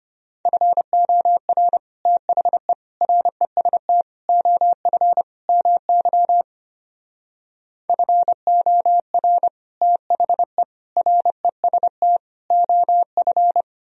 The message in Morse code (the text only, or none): FOR THE REST OF MY FOR THE REST OF